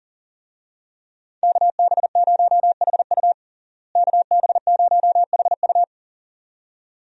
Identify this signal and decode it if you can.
Morse code, heard as a keyed tone, about 40 wpm, 700 Hz. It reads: KB0HU KB0HU
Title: USA Call Signs - 5 Characters - Repeated 2x - 40wpm